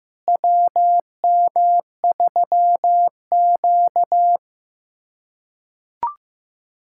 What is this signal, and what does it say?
Morse code, heard as a keyed tone, about 15 wpm, 700 Hz. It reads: WM3Q
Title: USA Callsign DE Callsign - 4 Characters - 15wpm